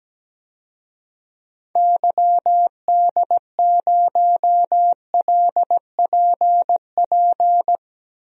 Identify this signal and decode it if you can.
Morse code, heard as a keyed tone, about 17 wpm, 700 Hz. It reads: YD0LPP
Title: DXCC-100 Call Signs - 6 Characters - Repeated 2x - 17wpm